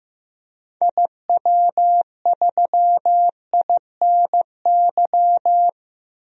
Morse code, heard as a keyed tone, about 15 wpm, 700 Hz. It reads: IW3INY
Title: DXCC-100 Call Signs - 6 Characters - 15wpm